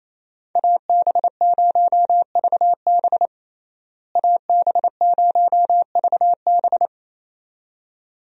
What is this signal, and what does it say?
Morse code, heard as a keyed tone, about 28 wpm, 700 Hz. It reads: AB0VB AB0VB
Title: USA Call Signs - 5 Characters - Repeated 2x - 28wpm